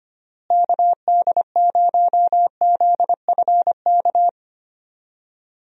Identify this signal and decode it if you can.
Morse code, heard as a keyed tone, about 25 wpm, 700 Hz. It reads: KD0ZFK
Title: USA Call Signs - 6 Characters - Repeated 3x - 25wpm